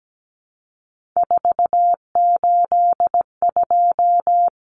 Morse code, heard as a keyed tone, about 17 wpm, 700 Hz. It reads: 482
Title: Common Cut Numbers - 3-Digit 17wpm